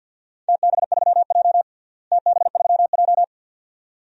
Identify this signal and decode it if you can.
Morse code, heard as a keyed tone, about 51 wpm, 700 Hz. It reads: TB3J TB3J
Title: DXCC-100 Call Signs - 4 Characters - Repeated 2x - 50wpm